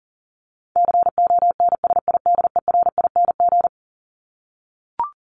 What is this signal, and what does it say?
Morse code, heard as a keyed tone, about 38 wpm, 700 Hz. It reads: CONSIDERING
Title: ING Suffix - Top 100 Words with Base Words 40wpm